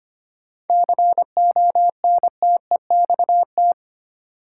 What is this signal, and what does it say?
Morse code, heard as a keyed tone, about 25 wpm, 700 Hz. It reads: CONTEXT
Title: Sets of 4 Words - New Word Drill - 25wpm